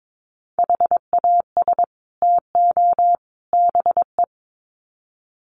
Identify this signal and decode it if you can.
Morse code, heard as a keyed tone, about 22 wpm, 700 Hz. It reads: HAS TO BE
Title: Sets of 3 Words 22wpm